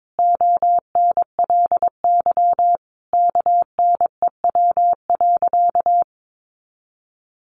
Morse code, heard as a keyed tone, about 22 wpm, 700 Hz. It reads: ONLY KNEW.
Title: Sentences from Top 500 Words 22wpm